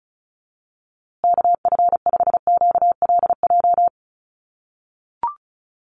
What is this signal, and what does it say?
Morse code, heard as a keyed tone, about 34 wpm, 700 Hz. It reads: KF5QLJ
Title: USA Call Signs Repeated 3x 35wpm